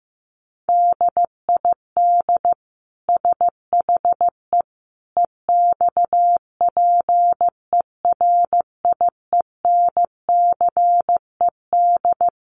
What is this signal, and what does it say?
Morse code, heard as a keyed tone, about 15 wpm, 700 Hz. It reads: DID SHE EXPERIENCED
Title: Sentences from Top 1100 Words 15wpm